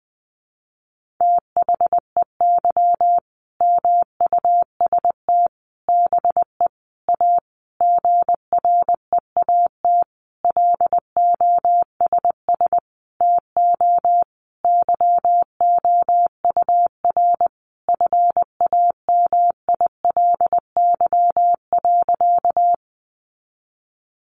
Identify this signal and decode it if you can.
Morse code, heard as a keyed tone, about 20 wpm, 700 Hz. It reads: THEY MUST BE A GREAT LOSS TO YOUR FAMILY.
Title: Sentences from Top 1400 Words 20wpm